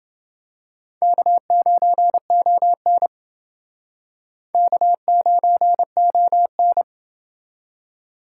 Morse code, heard as a keyed tone, about 28 wpm, 700 Hz. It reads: K9ON K9ON